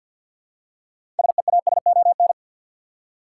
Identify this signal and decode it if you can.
Morse code, heard as a keyed tone, about 52 wpm, 700 Hz. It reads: SEASON
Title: Six Letter Words - 50wpm